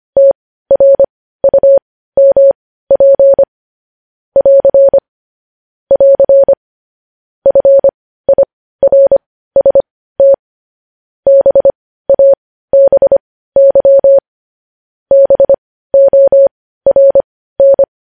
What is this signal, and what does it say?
Morse code, heard as a keyed tone, about 25 wpm, spaced as for 15 wpm, 550 Hz. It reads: TRUMP <AR> <AR> FIRST BABY BORN
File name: podcast / News headlines-1x-25x15